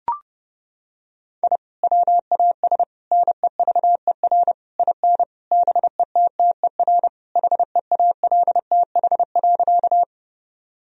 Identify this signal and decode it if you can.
Morse code, heard as a keyed tone, about 30 wpm, 700 Hz. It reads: I WAS NEVER IN BETTER HEALTH.
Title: Sentences from Top 1000 Words - Review Part 3 - 30wpm